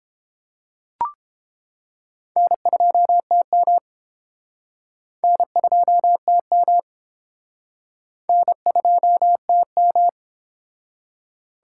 Morse code, heard as a keyed tone, about 30 wpm, 700 Hz. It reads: N2TM N2TM N2TM